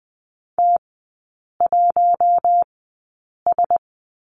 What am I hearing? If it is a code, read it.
Morse code, heard as a keyed tone, about 20 wpm, spaced as for 8 wpm, 700 Hz. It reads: T1S
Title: 041 Lesson 8e - Number 1 - Three Characters 20wpm